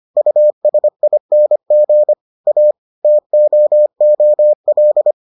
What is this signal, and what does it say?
Morse code, heard as a keyed tone, about 25 wpm, 600 Hz. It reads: USING A TOOL